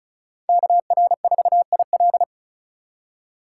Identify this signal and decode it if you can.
Morse code, heard as a keyed tone, about 34 wpm, 700 Hz. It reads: KR4IL
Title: USA Call Signs - 5 Characters - 3x Character Spacing - 35wpm